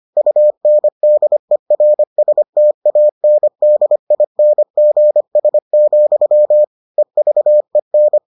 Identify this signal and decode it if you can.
Morse code, heard as a keyed tone, about 25 wpm, 600 Hz. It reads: UNDERSTANDINGS, EVEN